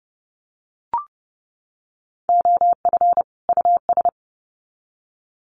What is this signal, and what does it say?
Morse code, heard as a keyed tone, about 30 wpm, 700 Hz. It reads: OF US
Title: Sets of 2 Words 30wpm